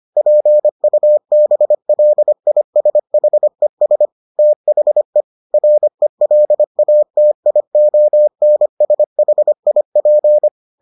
Morse code, heard as a keyed tone, about 25 wpm, 600 Hz. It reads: PUBLISHES THE RELATIONSHIP